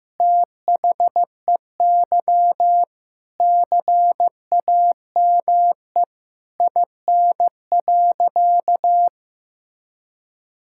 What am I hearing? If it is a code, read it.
Morse code, heard as a keyed tone, about 15 wpm, 700 Hz. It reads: THEY CAME IN.
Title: Sentences from Top 1000 Words - Review Part 5 - 15wpm